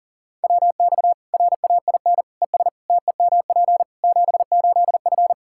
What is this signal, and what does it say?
Morse code, heard as a keyed tone, about 40 wpm, 700 Hz. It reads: WX RAIN ES TEMP 78F
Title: K1USN SST QSO Practice 40wpm